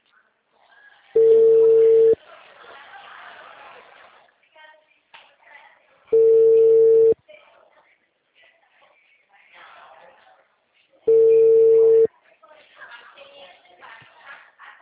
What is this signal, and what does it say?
Ringback tone (China, 450 Hz), with speech.
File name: out- -2012- - - .65972